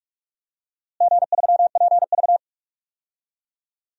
Morse code, heard as a keyed tone, about 46 wpm, 690 Hz. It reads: G3PV
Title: DXCC-100 Call Signs - 4 Characters - Repeated 3x - 45wpm